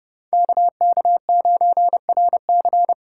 Morse code, heard as a keyed tone, about 34 wpm, 700 Hz. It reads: KK9RC